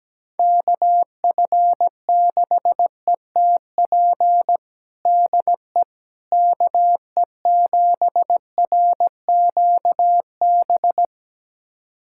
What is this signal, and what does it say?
Morse code, heard as a keyed tone, about 17 wpm, 700 Hz. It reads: KF6ETP DE KE7RQB
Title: USA Callsign DE Callsign - 6 Characters - 17wpm